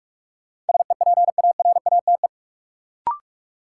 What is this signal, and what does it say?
Morse code, heard as a keyed tone, about 46 wpm, 700 Hz. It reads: SEPARATE
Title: Sentences from Top 1000 Words - Review Part 7 - New Word Drill - 45wpm